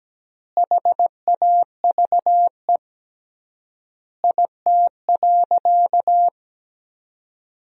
Morse code, heard as a keyed tone, about 17 wpm, 700 Hz. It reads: HAVE IT.
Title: 3x Word-Spacing - Sentences from Top 100 Words - 17wpm